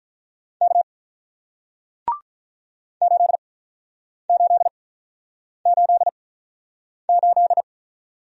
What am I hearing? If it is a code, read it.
Morse code, heard as a keyed tone, about 48 wpm, spaced as for 9 wpm, 700 Hz. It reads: K 8888